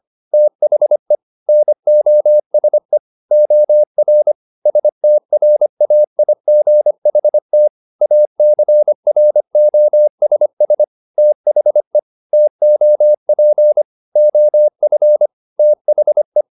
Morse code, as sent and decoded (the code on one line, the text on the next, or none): - .... / -. --- ... / --- .-. / ... - .-. .- .. --. .... - / .- -.-. .-. --- ... ... / - .... / - --- .--. / --- ..-. / - .... .
THE NOSE OR STRAIGHT ACROSS THE TOP OF THE